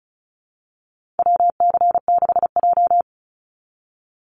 Morse code, heard as a keyed tone, about 34 wpm, 700 Hz. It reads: WC6J
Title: NA SOTA Activator Callsigns 35wpm